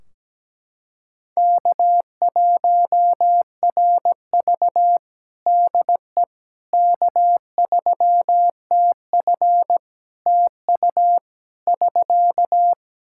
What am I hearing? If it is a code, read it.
Morse code, heard as a keyed tone, about 17 wpm, 700 Hz. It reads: K1RV DE K3TF TU <SK>
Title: Practice QSO - Exchange 4 - Ending - 17wpm